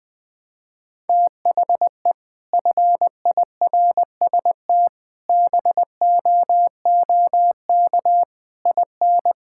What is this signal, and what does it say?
Morse code, heard as a keyed tone, about 20 wpm, 700 Hz. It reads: THE FIRST BOOK IN
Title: Sentences from Top 1000 Words - Review Part 5 - 20wpm